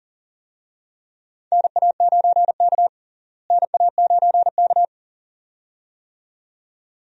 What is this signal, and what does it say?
Morse code, heard as a keyed tone, about 40 wpm, 700 Hz. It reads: NA9K NA9K
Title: USA Call Signs - 4 Characters - Repeated 2x - 40wpm